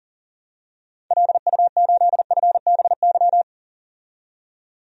Morse code, heard as a keyed tone, about 40 wpm, 700 Hz. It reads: LU8FBY
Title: DXCC-100 Call Signs - 6 Characters - 40wpm